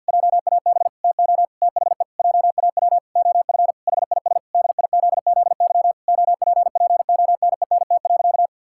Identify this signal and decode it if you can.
Morse code, heard as a keyed tone, about 52 wpm, 700 Hz. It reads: JAB TO THE JAW OF HIS DIZZY OPPONENT.
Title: Pangram Challenge v2 - 50wpm